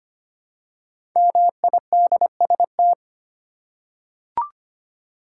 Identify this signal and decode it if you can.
Morse code, heard as a keyed tone, about 25 wpm, 700 Hz. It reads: MIDST
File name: 5 Letter Words - First 3 Letters Given - 25wpm